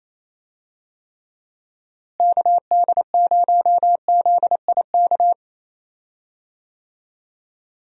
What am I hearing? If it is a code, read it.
Morse code, heard as a keyed tone, about 28 wpm, 700 Hz. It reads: KD0ZIK
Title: USA Call Signs - 6 Characters - Repeated 3x - 28wpm